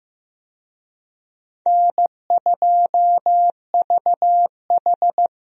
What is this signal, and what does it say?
Morse code, heard as a keyed tone, about 15 wpm, 700 Hz. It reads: N2VH
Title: USA Call Signs - 4 Characters - 15wpm